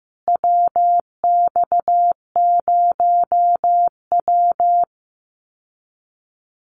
Morse code, heard as a keyed tone, about 15 wpm, 700 Hz. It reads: WX0W
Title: USA Call Signs Repeated 2x 15wpm